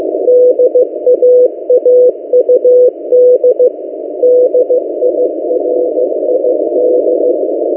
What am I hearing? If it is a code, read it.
Morse code, heard as a keyed tone, about 16 wpm, 500 Hz. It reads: DAAUD DIRID
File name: M08a- -1355utc-8096khz